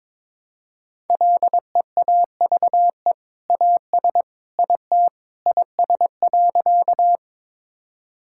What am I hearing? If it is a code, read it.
Morse code, heard as a keyed tone, about 22 wpm, 700 Hz. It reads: LEAVE AS IT IS.
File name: practice / 22wpm / Sentences with 2-Word Phrases - 22wpm